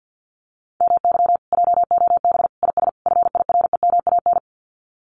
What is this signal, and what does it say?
Morse code, heard as a keyed tone, about 50 wpm, 700 Hz. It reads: MY JOB IS FIREMAN
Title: Common QSO Phrases Encore 50wpm